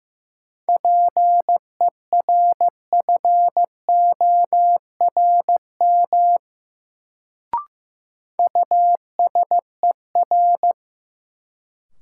Morse code, heard as a keyed tone, about 15 wpm, 700 Hz. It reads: PERFORM USER
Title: Sentences from Top 1000 Words - Review Part 5 - New Word Drill - 15wpm